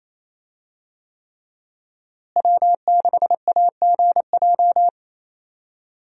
Morse code, heard as a keyed tone, about 28 wpm, 700 Hz. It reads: W6AGJ
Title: USA Call Signs - 5 Characters - Repeated 3x - 28wpm